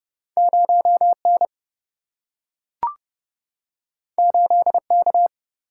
Morse code, heard as a keyed tone, about 30 wpm, 700 Hz. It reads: 0N 8K
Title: Single-Letter Single-Number 30wpm